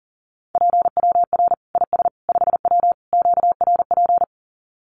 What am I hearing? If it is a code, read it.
Morse code, heard as a keyed tone, about 40 wpm, 700 Hz. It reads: PWR IS 5W QRP
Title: Common QSO Phrases 40wpm